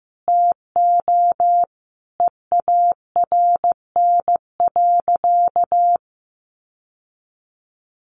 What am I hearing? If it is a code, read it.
Morse code, heard as a keyed tone, about 15 wpm, 700 Hz. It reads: TO EARN.
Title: Sentences from Top 1500 Words 15wpm